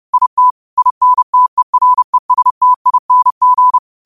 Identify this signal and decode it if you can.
Morse code, heard as a keyed tone, about 30 wpm, 1000 Hz. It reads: IT INTERESTING